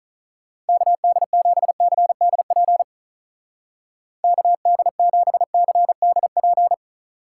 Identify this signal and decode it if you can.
Morse code, heard as a keyed tone, about 45 wpm, 700 Hz. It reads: KD7CDP KD7CDP